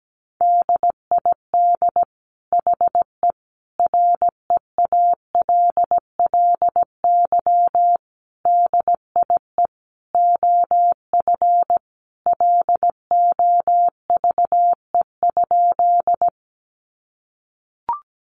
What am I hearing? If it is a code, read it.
Morse code, heard as a keyed tone, about 17 wpm, 700 Hz. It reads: DID HE REALLY DIE OF LOVE?
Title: Sentences from Top 1300 Words 17wpm